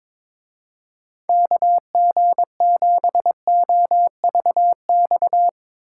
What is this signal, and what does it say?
Morse code, heard as a keyed tone, about 22 wpm, 700 Hz. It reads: KG7OVX
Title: Speed Racing - USA 2x3 Call Signs 22wpm